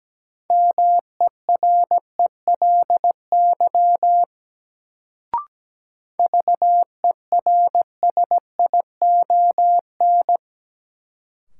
Morse code, heard as a keyed tone, about 17 wpm, 700 Hz. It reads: MERELY VERSION
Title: Sentences from Top 1400 Words - New Word Drill - 17wpm